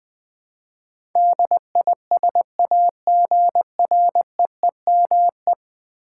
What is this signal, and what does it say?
Morse code, heard as a keyed tone, about 21 wpm, 700 Hz. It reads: DISAGREEME
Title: Build A Word - 5 Letters At A Time - 12-Letter Words - 20wpm